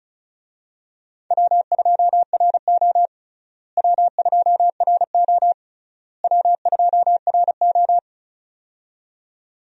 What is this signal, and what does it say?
Morse code, heard as a keyed tone, about 36 wpm, 700 Hz. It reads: W2RO W2RO W2RO
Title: USA Call Signs - 4 Characters - Repeated 3x - 35wpm